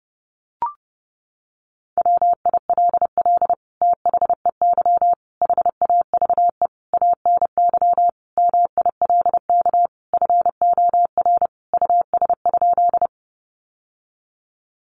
Morse code, heard as a keyed tone, about 30 wpm, 700 Hz. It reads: WILL THEY HAVE ANY MILK FOR US?
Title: Sentences from Top 1500 Words 30wpm